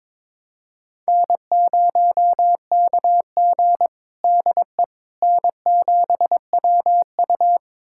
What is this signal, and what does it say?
Morse code, heard as a keyed tone, about 23 wpm, 700 Hz. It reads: N0KG DE N7WU
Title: USA Callsign DE Callsign - 4 Characters - 22wpm